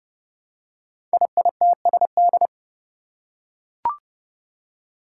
Morse code, heard as a keyed tone, about 30 wpm, 700 Hz. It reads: IITSD
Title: Code Groups - 5 Letters - 30wpm